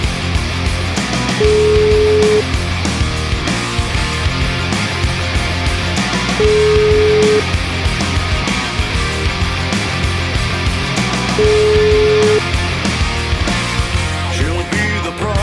Ringback tone (Europe, 425 Hz), with noise.